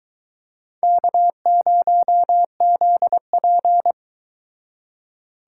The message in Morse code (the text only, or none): K0ZP